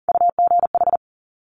Morse code, heard as a keyed tone, about 42 wpm, 690 Hz. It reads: UGH
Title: Top Three Letter Combinations in Words 40wpm